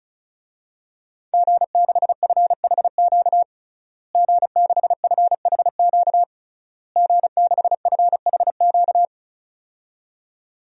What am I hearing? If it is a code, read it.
Morse code, heard as a keyed tone, about 36 wpm, 700 Hz. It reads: G6FHQ G6FHQ G6FHQ
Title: DXCC-100 Call Signs - 5 Characters - Repeated 3x - 35wpm